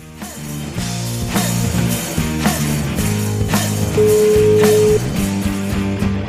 Ringback tone (Europe, 425 Hz), with noise.